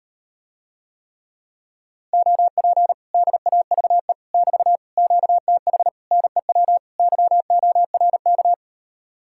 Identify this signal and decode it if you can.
Morse code, heard as a keyed tone, about 38 wpm, 700 Hz. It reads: OP DAVE <BT> QTH NEW YORK